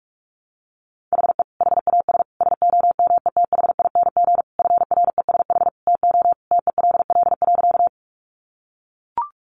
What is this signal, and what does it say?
Morse code, heard as a keyed tone, about 44 wpm, 700 Hz. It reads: HE HAS SOMETHING FRESH TO TELL.
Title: Sentences from Top 1200 Words 45wpm